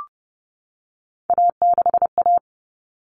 Morse code, heard as a keyed tone, about 30 wpm, 700 Hz. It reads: A6A
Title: Common Cut Numbers - 3-Digit 30wpm